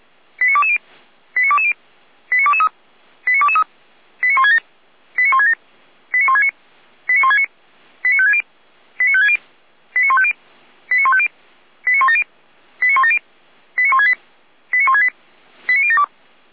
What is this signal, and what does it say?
Five-tone selective call (ZVEI), heard as quick runs of five tones, tones of about 70 ms. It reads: 89220 89220 89223 89223 89167 89167 89169 89169 89560 89560 89150 89150 89170 89170 89168 89168 89072